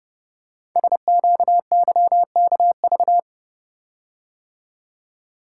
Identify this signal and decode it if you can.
Morse code, heard as a keyed tone, about 30 wpm, 700 Hz. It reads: SQYKV
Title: Code Groups - 5 Letters - 30wpm